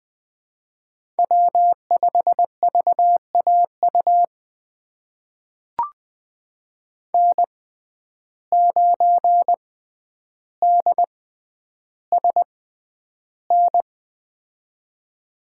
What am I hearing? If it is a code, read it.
Morse code, heard as a keyed tone, about 20 wpm, spaced as for 10 wpm, 700 Hz. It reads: W5VAU N9DSN
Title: USA Call Signs - 5 Characters - 6x Character Spacing - 20wpm